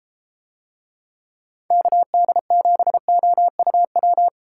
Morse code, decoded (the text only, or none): KD7OUW